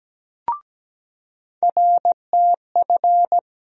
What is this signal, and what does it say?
Morse code, heard as a keyed tone, about 17 wpm, 700 Hz. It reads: RTF
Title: USA Call Sign Postfixes - 3 Letters 17wpm